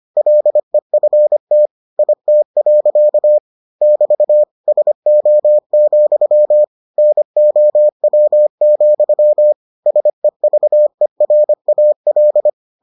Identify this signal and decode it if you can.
Morse code, heard as a keyed tone, about 25 wpm, 600 Hz. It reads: LEFT IT. <BT> SO, NOW, SEVERAL